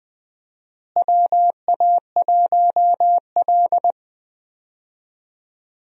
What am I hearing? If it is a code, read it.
Morse code, heard as a keyed tone, about 21 wpm, 700 Hz. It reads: WA1L
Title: USA Call Signs - 4 Characters - Repeated 2x - 20wpm